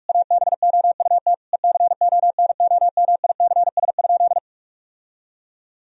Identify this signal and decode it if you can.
Morse code, heard as a keyed tone, about 46 wpm, 700 Hz. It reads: ABOUT ECONOMICS?
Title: Sentences from Top 1000 Words - Review Part 3 - 45wpm